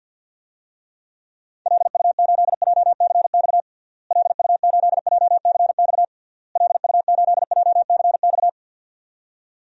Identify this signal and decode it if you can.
Morse code, heard as a keyed tone, about 51 wpm, 700 Hz. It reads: LU8JCX LU8JCX LU8JCX
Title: DXCC-100 Call Signs - 6 Characters - Repeated 3x - 50wpm